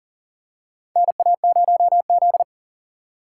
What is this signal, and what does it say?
Morse code, heard as a keyed tone, about 40 wpm, 700 Hz. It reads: NA0Z
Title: USA Call Signs - 4 Characters - 4x Character Spacing - 40wpm